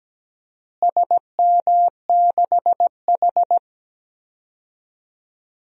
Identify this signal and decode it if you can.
Morse code, heard as a keyed tone, about 17 wpm, 700 Hz. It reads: SM6H
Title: DXCC-100 Call Signs - 4 Characters - Repeated 2x - 17wpm